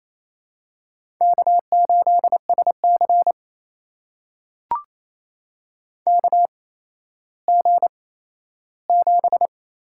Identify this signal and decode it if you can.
Morse code, heard as a keyed tone, about 28 wpm, spaced as for 14 wpm, 700 Hz. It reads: K8SC KG7
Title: USA Call Signs - 8x Character Spacing - 28wpm